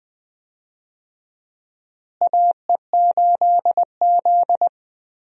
Morse code, heard as a keyed tone, about 21 wpm, 700 Hz. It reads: AE8Z
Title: USA Call Signs - 4 Characters - Repeated 2x - 20wpm